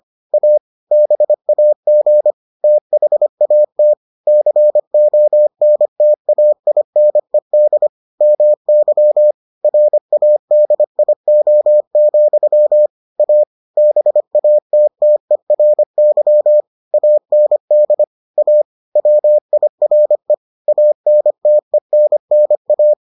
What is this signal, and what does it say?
Morse code, heard as a keyed tone, about 25 wpm, 600 Hz. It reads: A BAG THAT CONTAINED MY RADIO, A BATTERY AND A WIRE ANTENNA